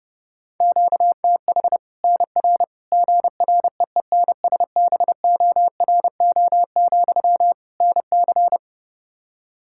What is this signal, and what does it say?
Morse code, heard as a keyed tone, about 30 wpm, 700 Hz. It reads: QTH NR GREENSBORO, NC
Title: Practice QSO - Exchange 1 - Beginning - 30wpm